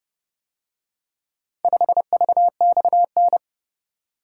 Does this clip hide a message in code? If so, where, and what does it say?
Yes: Morse code from 1.6 to 3.4 s, heard as a keyed tone, about 30 wpm, 700 Hz. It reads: 5VXN